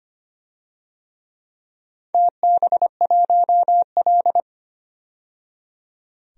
Morse code, heard as a keyed tone, about 25 wpm, 700 Hz. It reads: TB1L
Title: DXCC-100 Call Signs - 4 Characters - 25wpm